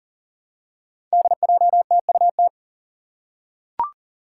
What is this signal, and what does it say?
Morse code, heard as a keyed tone, about 40 wpm, 700 Hz. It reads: DJTUT
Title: Five Challenging Letter-Number Combinations 40wpm